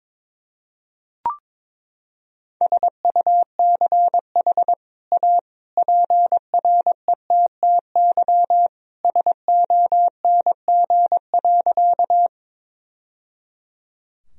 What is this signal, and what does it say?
Morse code, heard as a keyed tone, about 22 wpm, 700 Hz. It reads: SUCH A PRETTY SONG.
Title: Sentences from Top 1000 Words - Review Part 2 - 22wpm